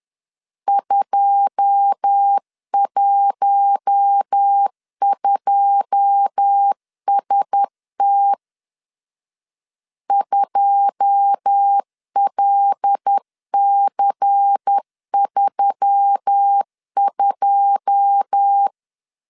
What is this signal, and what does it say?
Morse code, heard as a keyed tone, about 10 wpm, 800 Hz. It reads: 212ST 2LC32
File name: LEZIONI / 9 2 10wpm 2736